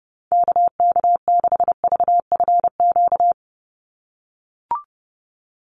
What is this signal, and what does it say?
Morse code, heard as a keyed tone, about 30 wpm, 700 Hz. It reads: KK6VFQ
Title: USA Call Signs 30wpm